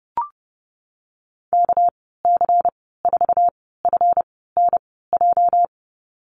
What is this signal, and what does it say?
Morse code, heard as a keyed tone, about 30 wpm, spaced as for 17 wpm, 700 Hz. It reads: KC4FNJ
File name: Lesson 37g - Letter x - Call Signs 30wpm